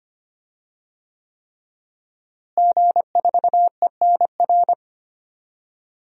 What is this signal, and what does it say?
Morse code, heard as a keyed tone, about 25 wpm, 700 Hz. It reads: G4ENR
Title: DXCC-100 Call Signs - 5 Characters - Repeated 3x - 25wpm